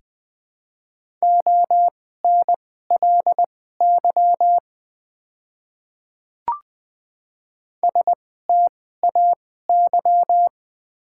Morse code, heard as a keyed tone, about 20 wpm, spaced as for 15 wpm, 700 Hz. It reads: ONLY STAY
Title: 2x Character Spacing - Top 500 Words - 20wpm